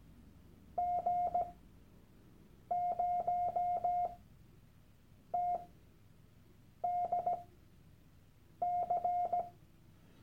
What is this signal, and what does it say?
Morse code, heard as a keyed tone, about 17 wpm, spaced as for 6 wpm, 710 Hz. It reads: G0TDC
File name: Workshop Dateien / CW-Online-Kurs / Lektion 001